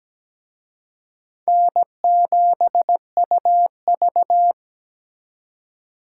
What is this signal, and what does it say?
Morse code, heard as a keyed tone, about 20 wpm, 700 Hz. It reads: N7UV
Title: Speed Racing - USA 1x2 Call Signs 17wpm